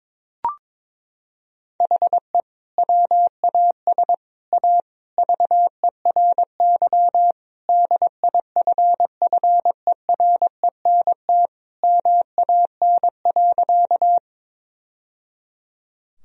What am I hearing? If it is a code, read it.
Morse code, heard as a keyed tone, about 22 wpm, 700 Hz. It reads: HE WAS A VERY DIFFERENT MAN.
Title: Sentences from Top 1000 Words - Review Part 3 - 22wpm